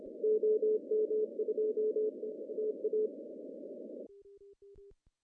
Morse code, heard as a keyed tone, about 27 wpm, spaced as for 21 wpm, 430 Hz. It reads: OM2TA
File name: CW / cqww-cw-k5tr